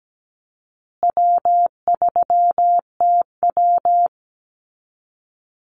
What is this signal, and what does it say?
Morse code, heard as a keyed tone, about 17 wpm, 700 Hz. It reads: W3TW
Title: Easy Call Signs Repeated 3x - 17wpm